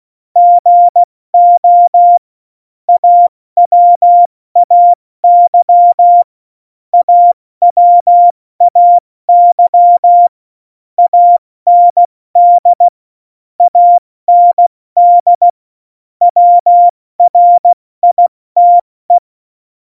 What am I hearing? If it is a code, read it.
Morse code, heard as a keyed tone, about 16 wpm, 700 Hz. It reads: GO AWAY AWAY AND AND WRITE
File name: Morsekurs / Witze / Witz 25 Juli 17, Tempo 70 F80